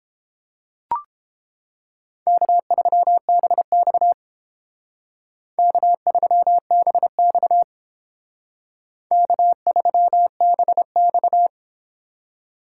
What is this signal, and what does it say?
Morse code, heard as a keyed tone, about 28 wpm, 700 Hz. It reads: K3BX K3BX K3BX